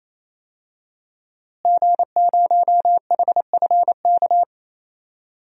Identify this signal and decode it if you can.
Morse code, heard as a keyed tone, about 28 wpm, 700 Hz. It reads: G0HFK
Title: DXCC-100 Call Signs - 5 Characters - Repeated 3x - 28wpm